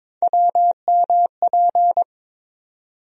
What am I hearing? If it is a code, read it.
Morse code, heard as a keyed tone, about 22 wpm, 700 Hz. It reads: WMP
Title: USA Call Sign Postfixes - 3 Letters 22wpm